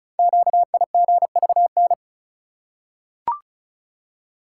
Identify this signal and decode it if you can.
Morse code, heard as a keyed tone, about 36 wpm, 690 Hz. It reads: QIGVN
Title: Code Groups - 5 Letters - 35wpm